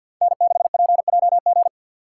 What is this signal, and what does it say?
Morse code, heard as a keyed tone, about 51 wpm, 700 Hz. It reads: N6PJG
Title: USA Call Signs - 5 Characters - 6x Character Spacing - 50wpm